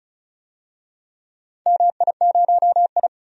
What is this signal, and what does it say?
Morse code, heard as a keyed tone, about 36 wpm, 700 Hz. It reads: MI0I
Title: DXCC-100 Call Signs - 4 Characters - 35wpm